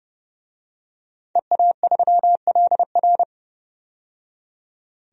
Morse code, heard as a keyed tone, about 30 wpm, 700 Hz. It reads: EA3LR
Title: DXCC-100 Call Signs - 5 Characters - 30wpm